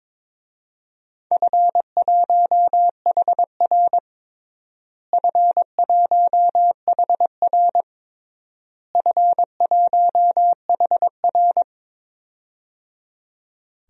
Morse code, heard as a keyed tone, about 22 wpm, 700 Hz. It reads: F1HR F1HR F1HR
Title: DXCC-100 Call Signs - 4 Characters - Repeated 3x - 22wpm